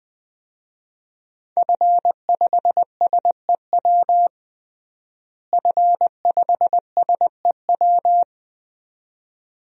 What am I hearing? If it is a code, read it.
Morse code, heard as a keyed tone, about 21 wpm, 700 Hz. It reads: F5SEW F5SEW
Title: DXCC-100 Call Signs - 5 Characters - Repeated 2x - 20wpm